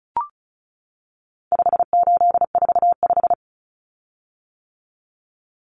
Morse code, heard as a keyed tone, about 34 wpm, 700 Hz. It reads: 5845
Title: Common Cut Numbers - 4-Digit 35wpm